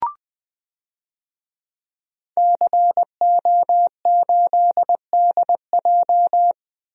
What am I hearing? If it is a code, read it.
Morse code, heard as a keyed tone, about 20 wpm, 700 Hz. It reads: CO8DJ
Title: DXCC-100 Call Signs - 5 Characters - Repeated 3x - 20wpm